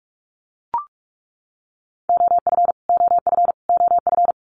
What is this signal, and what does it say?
Morse code, heard as a keyed tone, about 44 wpm, 700 Hz. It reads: OF OF OF
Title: Top 200 Words Repeated 45wpm